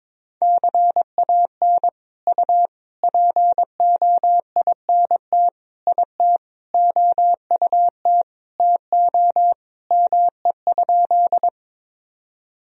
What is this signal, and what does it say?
Morse code, heard as a keyed tone, about 22 wpm, 700 Hz. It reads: CAN U POINT IT OUT TO ME?
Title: Sentences from Top 200 Words and Abbreviations 22wpm